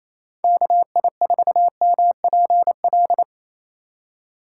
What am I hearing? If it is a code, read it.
Morse code, heard as a keyed tone, about 28 wpm, 700 Hz. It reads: KI4MPL
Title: USA Call Signs - 6 Characters - 4x Character Spacing - 28wpm